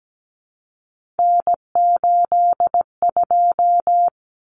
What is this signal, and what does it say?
Morse code, heard as a keyed tone, about 17 wpm, 700 Hz. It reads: N82
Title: Common Cut Numbers - 3-Digit 17wpm